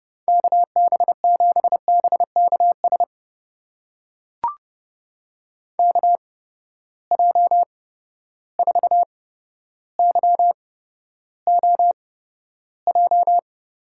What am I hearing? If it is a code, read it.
Morse code, heard as a keyed tone, about 30 wpm, spaced as for 13 wpm, 700 Hz. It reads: KB7BKS KJ4YOJ
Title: USA Call Signs - 6 Characters - 8x Character Spacing - 30wpm